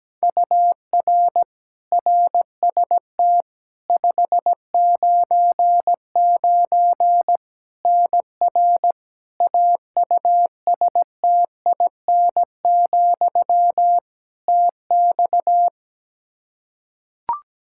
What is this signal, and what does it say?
Morse code, heard as a keyed tone, about 17 wpm, 700 Hz. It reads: UR RST 599 NR AUSTIN, TX
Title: Practice QSO - Exchange 1 - Beginning - 17wpm